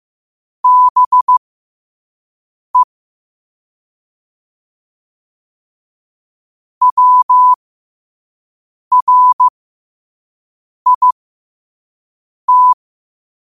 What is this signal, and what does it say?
Morse code, heard as a keyed tone, about 15 wpm, spaced as for 5 wpm, 1000 Hz. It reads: BE WRIT
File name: qotdfiles / 5wpm